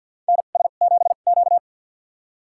Morse code, heard as a keyed tone, about 52 wpm, spaced as for 37 wpm, 700 Hz. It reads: NS7Q
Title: USA Call Signs - 4 Characters - 2x Character Spacing - 50wpm